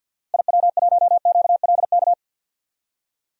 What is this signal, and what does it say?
Morse code, heard as a keyed tone, about 50 wpm, 700 Hz. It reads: IW1QLK
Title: DXCC-100 Call Signs - 6 Characters - Repeated 2x - 50wpm